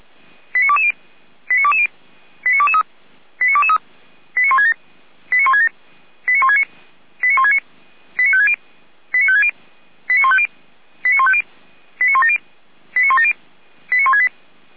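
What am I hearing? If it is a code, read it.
Five-tone selective call (ZVEI), heard as quick runs of five tones, tones of about 70 ms. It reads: 89220 89220 89223 89223 89167 89167 89169 89169 89560 89560 89150 89150 89170 89170 89168
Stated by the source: ZVEI five-tone call received by SDR.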